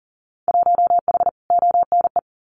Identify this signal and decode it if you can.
Morse code, heard as a keyed tone, about 40 wpm, 700 Hz. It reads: 1H ONE
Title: Winter Field Day Exchanges Encore 40wpm